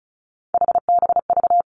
Morse code, heard as a keyed tone, about 36 wpm, 700 Hz. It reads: HBV
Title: Three Challenging Letter-Number Combinations 35wpm